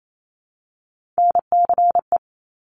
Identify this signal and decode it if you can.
Morse code, heard as a keyed tone, about 28 wpm, 700 Hz. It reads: NCE
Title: Top Three Letter Combinations in Words 28wpm